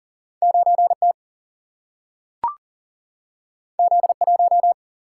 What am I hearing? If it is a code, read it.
Morse code, heard as a keyed tone, about 40 wpm, 700 Hz. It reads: 9T Z1
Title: Single-Letter Single-Number 40wpm